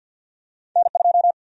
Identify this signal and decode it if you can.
Morse code, heard as a keyed tone, about 52 wpm, 700 Hz. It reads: N2
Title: USA Call Sign Prefixes - 50wpm